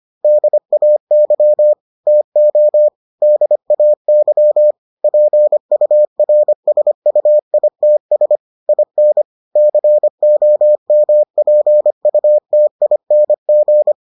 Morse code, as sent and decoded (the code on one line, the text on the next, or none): -.. .- -.-- / - --- / -.. .- -.-- / .--. ..- .-. ... ..- .. - ... / .. -. / -.-. --- -- .--. ..- - .. -. --.
DAY TO DAY PURSUITS IN COMPUTING